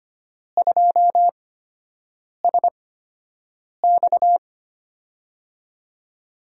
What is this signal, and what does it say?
Morse code, heard as a keyed tone, about 25 wpm, spaced as for 7 wpm, 700 Hz. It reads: 2SX
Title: USA Call Signs - 4 Characters - 8x Character Spacing - 25wpm